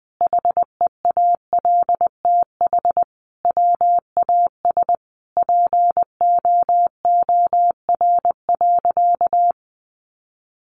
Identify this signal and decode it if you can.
Morse code, heard as a keyed tone, about 20 wpm, 700 Hz. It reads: HEALTH WAS POOR.
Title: Sentences from Top 1000 Words 20wpm